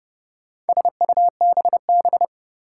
Morse code, heard as a keyed tone, about 30 wpm, 700 Hz. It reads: SUBB
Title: Four Challenging Letter-Number Combinations 30wpm